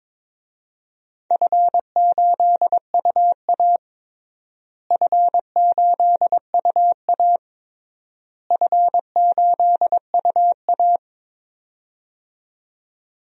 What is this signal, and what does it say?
Morse code, heard as a keyed tone, about 22 wpm, 700 Hz. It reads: F8UA F8UA F8UA
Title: DXCC-100 Call Signs - 4 Characters - Repeated 3x - 22wpm